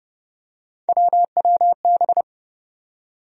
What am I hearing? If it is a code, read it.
Morse code, heard as a keyed tone, about 31 wpm, 700 Hz. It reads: WWB
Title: USA Call Sign Postfixes - 3 Letters 30wpm